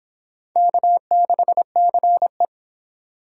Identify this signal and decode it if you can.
Morse code, heard as a keyed tone, about 26 wpm, 700 Hz. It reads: K6CE